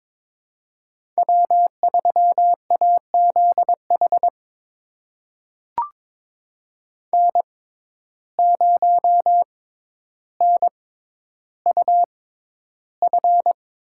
Morse code, heard as a keyed tone, about 22 wpm, spaced as for 11 wpm, 700 Hz. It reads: W3AZH N0NUF
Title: USA Call Signs - 5 Characters - 6x Character Spacing - 22wpm